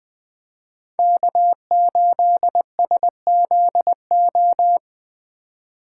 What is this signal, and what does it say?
Morse code, heard as a keyed tone, about 20 wpm, 700 Hz. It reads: K8SZO
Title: USA Call Signs - 5 Characters - 3x Character Spacing - 20wpm